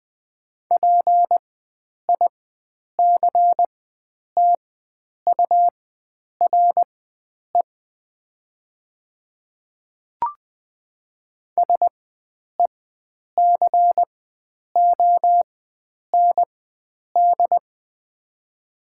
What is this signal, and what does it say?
Morse code, heard as a keyed tone, about 20 wpm, spaced as for 9 wpm, 700 Hz. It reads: PICTURE SECOND
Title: 4x Character Spacing - Top 500 Words - 20wpm